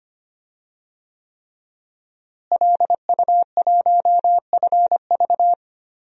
Morse code, heard as a keyed tone, about 25 wpm, 700 Hz. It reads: LU1FV
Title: DXCC-100 Call Signs - 5 Characters - Repeated 3x - 25wpm